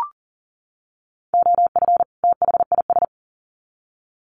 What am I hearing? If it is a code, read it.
Morse code, heard as a keyed tone, about 40 wpm, 700 Hz. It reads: OF THIS
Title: Sets of 2 Words 40wpm